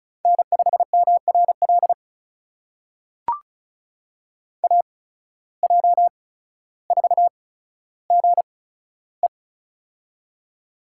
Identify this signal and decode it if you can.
Morse code, heard as a keyed tone, about 36 wpm, spaced as for 15 wpm, 700 Hz. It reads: N5MRL AJ4GE
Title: USA Call Signs - 5 Characters - 8x Character Spacing - 35wpm